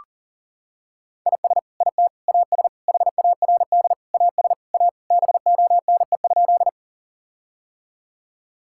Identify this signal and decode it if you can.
Morse code, heard as a keyed tone, about 40 wpm, 700 Hz. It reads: IS IT AS HARD AS A BONE?
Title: Sentences from Top 1600 Words 40wpm